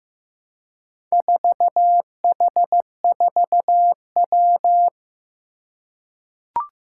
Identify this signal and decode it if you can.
Morse code, heard as a keyed tone, about 15 wpm, 700 Hz. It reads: 4H4W
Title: Four Challenging Letter-Number Combinations 15wpm